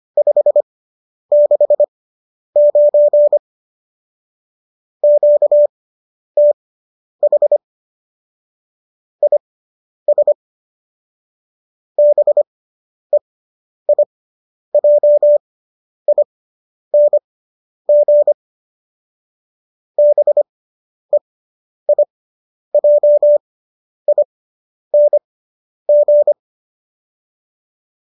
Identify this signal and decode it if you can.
Morse code, heard as a keyed tone, about 25 wpm, spaced as for 10 wpm, 590 Hz. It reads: 569 QTH IS BEIJING BEIJING